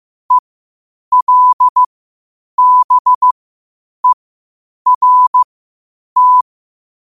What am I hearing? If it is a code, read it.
Morse code, heard as a keyed tone, about 15 wpm, spaced as for 8 wpm, 1000 Hz. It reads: ELBERT
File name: qotdfiles / 8wpm